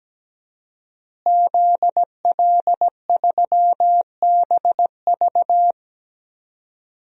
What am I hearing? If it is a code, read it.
Morse code, heard as a keyed tone, about 17 wpm, 700 Hz. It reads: ZL3BV
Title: DXCC-100 Call Signs - 5 Characters - Repeated 3x - 17wpm